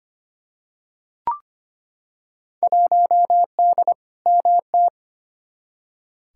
Morse code, heard as a keyed tone, about 25 wpm, 700 Hz. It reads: 1D MT
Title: Field Day Exchanges - 25wpm